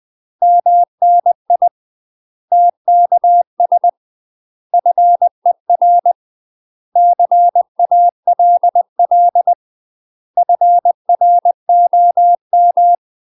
Morse code, heard as a keyed tone, about 20 wpm, 700 Hz. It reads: MNI TKS FER CALL FROM
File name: UK MASTER QSOs TRAINING / 20WPM / UK MASTER QSOs 0040